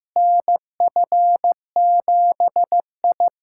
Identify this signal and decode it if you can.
Morse code, heard as a keyed tone, about 14 wpm, 700 Hz. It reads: NF7I
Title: USA Callsign DE Callsign - 4 Characters - 15wpm